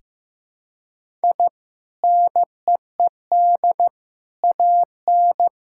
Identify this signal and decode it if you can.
Morse code, heard as a keyed tone, about 15 wpm, 700 Hz. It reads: I NEED AN
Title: Sentences from Top 1000 Words - Review Part 5 - 15wpm